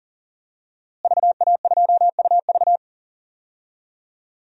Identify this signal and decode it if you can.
Morse code, heard as a keyed tone, about 40 wpm, 700 Hz. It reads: VA2UV
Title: DXCC-100 Call Signs - 5 Characters - Repeated 2x - 40wpm